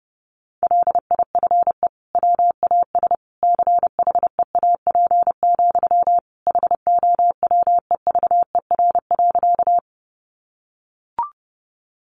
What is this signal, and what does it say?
Morse code, heard as a keyed tone, about 30 wpm, 700 Hz. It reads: LIFE WAS CHEAP, HOWEVER.
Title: Sentences from Top 1200 Words 30wpm